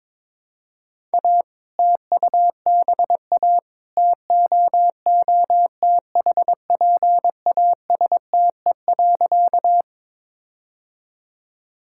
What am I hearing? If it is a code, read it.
Morse code, heard as a keyed tone, about 21 wpm, 700 Hz. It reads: A TUBA TOOTHPASTE.
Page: morse-code-ninja-practice-22wpm